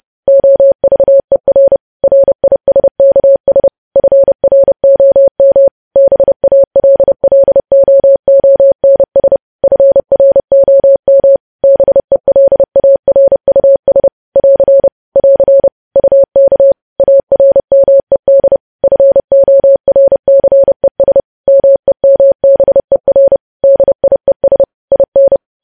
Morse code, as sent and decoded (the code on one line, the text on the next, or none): --- ...- . .-. / .-. .. ... -.- ... / ..-. .-. --- -- / -... .- .-.. .-.. --- --- -. ... / ..-. .-. --- -- / -... .-.. .- .-. ..- ... / .-.-. / .-.-. / ..- -.- / .- .-. -- . -.. / ..-. --- .-. -.-. ... / -- . -- -... .-. / -.. .. ... / .. -.
OVER RISKS FROM BALLOONS FROM BELARUS <AR> <AR> UK ARMED FORCES MEMBER DIES IN